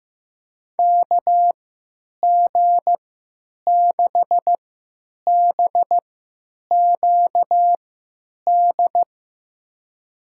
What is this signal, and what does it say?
Morse code, heard as a keyed tone, about 15 wpm, spaced as for 9 wpm, 700 Hz. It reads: KG6BQD
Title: USA Call Signs - 6 Characters - 3x Character Spacing - 15wpm